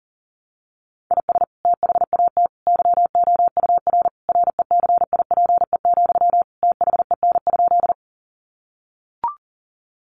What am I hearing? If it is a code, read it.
Morse code, heard as a keyed tone, about 40 wpm, 700 Hz. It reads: IS THAT YOUR RECIPE, THEN?
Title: Sentences from Top 1200 Words 40wpm